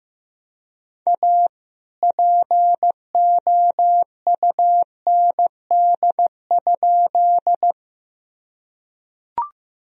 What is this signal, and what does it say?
Morse code, heard as a keyed tone, about 15 wpm, 700 Hz. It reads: A POUND?
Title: Sentences from Top 1000 Words - Review Part 2 - 15wpm